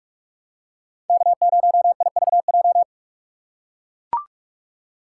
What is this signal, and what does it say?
Morse code, heard as a keyed tone, about 44 wpm, 700 Hz. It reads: K0IVJ